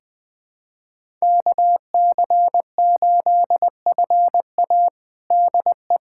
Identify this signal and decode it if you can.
Morse code, heard as a keyed tone, about 20 wpm, 700 Hz. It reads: KC8FA DE
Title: USA Callsign DE Callsign - 5 Characters - 20wpm